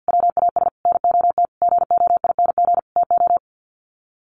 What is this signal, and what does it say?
Morse code, heard as a keyed tone, about 48 wpm, 700 Hz. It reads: WAS NOT GOING TO
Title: Sets of 5 Words Encore2 50wpm